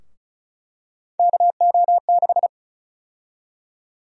Morse code, heard as a keyed tone, about 36 wpm, 700 Hz. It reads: KO6
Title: USA Call Sign Prefixes - Repeated 2x - 35wpm